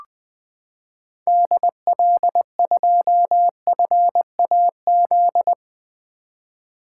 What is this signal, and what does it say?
Morse code, heard as a keyed tone, about 20 wpm, 700 Hz. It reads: DL2FAZ
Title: 2023 SOTA Activator Call Signs - 20wpm